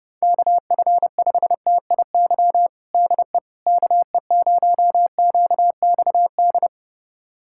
Morse code, heard as a keyed tone, about 32 wpm, 700 Hz. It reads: KF5TIY DE KE0QXD
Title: USA Callsign DE Callsign - 6 Characters - 30wpm